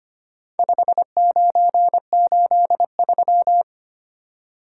Morse code, heard as a keyed tone, about 25 wpm, 700 Hz. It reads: 5983
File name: practice / 25wpm / Four Letter-Number Combinations - 25wpm